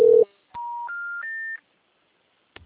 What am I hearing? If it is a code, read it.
Special information tone.
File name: out- -2009- - - .16